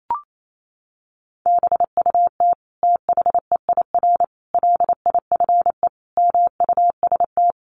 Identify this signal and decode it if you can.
Morse code, heard as a keyed tone, about 28 wpm, 700 Hz. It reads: BUT THEIR LIFE MUST
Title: Sentences from Top 1400 Words 28wpm